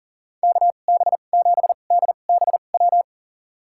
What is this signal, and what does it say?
Morse code, heard as a keyed tone, about 40 wpm, spaced as for 29 wpm, 700 Hz. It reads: KB7DBW
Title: USA Call Signs - 2x Character Spacing - 40wpm